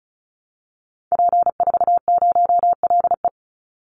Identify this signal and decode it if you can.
Morse code, heard as a keyed tone, about 34 wpm, 700 Hz. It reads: P40LE
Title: Call Signs Repeated 2x - 35wpm